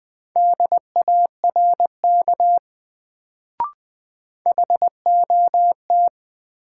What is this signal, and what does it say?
Morse code, heard as a keyed tone, about 20 wpm, 700 Hz. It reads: DARK HOT
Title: Sentences from Top 500 Words - New Word Drill - 20wpm